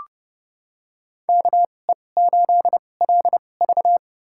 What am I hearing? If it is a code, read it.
Morse code, heard as a keyed tone, about 29 wpm, spaced as for 21 wpm, 700 Hz. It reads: KE8LV
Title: USA Call Signs - 5 Characters - 2x Character Spacing - 30wpm